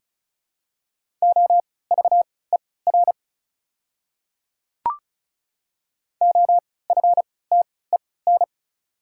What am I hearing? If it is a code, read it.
Morse code, heard as a keyed tone, about 34 wpm, spaced as for 20 wpm, 700 Hz. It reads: OVER OFTEN
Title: 3x Character Spacing - Top 500 Words - 35wpm